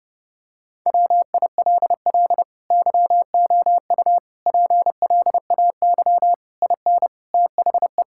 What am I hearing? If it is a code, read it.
Morse code, heard as a keyed tone, about 30 wpm, 700 Hz. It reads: WILL YOU PLAY IN THE
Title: Sentences from Top 1600 Words 30wpm